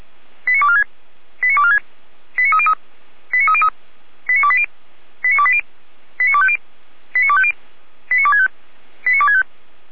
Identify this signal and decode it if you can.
Five-tone selective call (ZVEI), heard as quick runs of five tones, tones of about 70 ms. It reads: 89246 89246 89303 89303 89280 89280 89250 89250 89265 89265